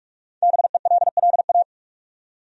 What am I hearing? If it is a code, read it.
Morse code, heard as a keyed tone, about 47 wpm, 700 Hz. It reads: BELLA
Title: USA Common Names - 45wpm